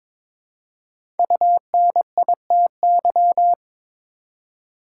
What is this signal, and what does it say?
Morse code, heard as a keyed tone, about 22 wpm, 700 Hz. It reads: UNITY
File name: 5 Letter Words - First 3 Letters Given - 22wpm